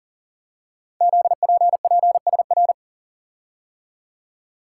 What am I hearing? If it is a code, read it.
Morse code, heard as a keyed tone, about 42 wpm, 700 Hz. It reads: ZPPSR
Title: Code Groups - 5 Letters - 40wpm